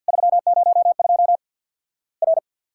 Morse code, heard as a keyed tone, about 50 wpm, 700 Hz. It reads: 302 R
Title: Worked All Europe DX Contest - QTC Exchange with Repeats - 50wpm